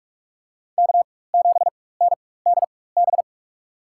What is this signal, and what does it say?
Morse code, heard as a keyed tone, about 43 wpm, spaced as for 21 wpm, 700 Hz. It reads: K7NDB
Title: USA Call Signs - 5 Characters - 4x Character Spacing - 45wpm